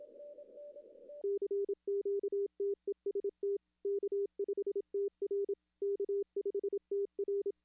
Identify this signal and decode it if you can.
Morse code, heard as a keyed tone, about 27 wpm, 400 Hz. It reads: CQTEST K5TR K5TR
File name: contests / arrl-160-k5tr-op-w5tn -0400